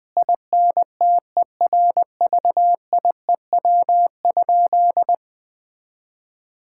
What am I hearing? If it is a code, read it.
Morse code, heard as a keyed tone, about 20 wpm, 700 Hz. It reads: INTERVIEW?
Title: Sentences from Top 1600 Words 20wpm